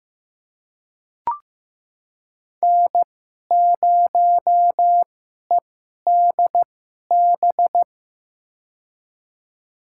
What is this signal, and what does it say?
Morse code, heard as a keyed tone, about 15 wpm, spaced as for 11 wpm, 700 Hz. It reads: N0EDB
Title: USA Call Signs - 5 Characters - 2x Character Spacing - 15wpm